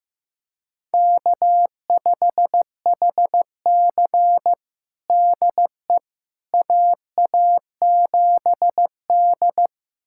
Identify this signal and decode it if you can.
Morse code, heard as a keyed tone, about 15 wpm, 700 Hz. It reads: K5HC DE AA7D
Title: USA Callsign DE Callsign - 4 Characters - 15wpm